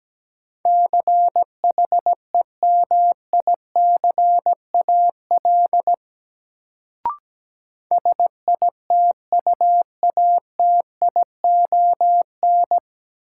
Morse code, heard as a keyed tone, about 17 wpm, 700 Hz. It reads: CHEMICAL SITUATION
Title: Sentences from Top 1200 Words - New Word Drill Alt - 17wpm